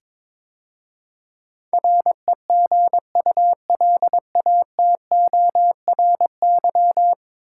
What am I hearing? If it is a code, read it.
Morse code, heard as a keyed tone, about 22 wpm, 700 Hz. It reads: REGULATORY
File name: practice / 22wpm / 10 Letter Words - First 3 Letters Given - 22wpm